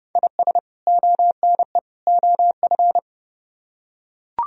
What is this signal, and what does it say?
Morse code, heard as a keyed tone, about 30 wpm, 700 Hz. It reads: IS ONE OF